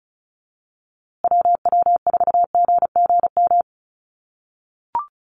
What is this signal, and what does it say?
Morse code, heard as a keyed tone, about 34 wpm, 700 Hz. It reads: WW4GGM
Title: USA Call Signs Repeated 2x 35wpm